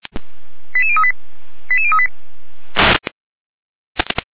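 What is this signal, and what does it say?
Five-tone selective call (ZVEI), heard as quick runs of five tones, tones of about 80 ms. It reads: 88038 88038
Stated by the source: ZVEI five-tone selective call received via SDR.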